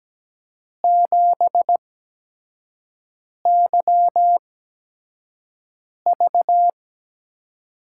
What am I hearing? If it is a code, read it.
Morse code, heard as a keyed tone, about 17 wpm, spaced as for 5 wpm, 700 Hz. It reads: 7YV
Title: USA Call Signs - 4 Characters - 8x Character Spacing - 17wpm